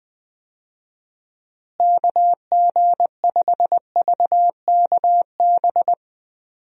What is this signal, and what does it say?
Morse code, heard as a keyed tone, about 20 wpm, 700 Hz. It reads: KG5VKB
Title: USA Call Signs - 6 Characters - Repeated 3x - 20wpm